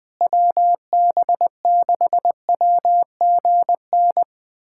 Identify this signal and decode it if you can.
Morse code, heard as a keyed tone, about 20 wpm, 700 Hz. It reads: WB6WGN
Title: USA Callsign DE Callsign - 6 Characters - 20wpm